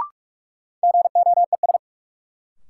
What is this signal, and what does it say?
Morse code, heard as a keyed tone, about 42 wpm, 700 Hz. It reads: GOES
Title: Sentences from Top 1000 Words - Review Part 5 - New Word Drill - 45wpm